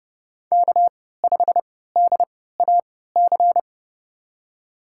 Morse code, heard as a keyed tone, about 30 wpm, spaced as for 17 wpm, 700 Hz. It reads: K5DAC
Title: USA Call Signs - 5 Characters - 3x Character Spacing - 30wpm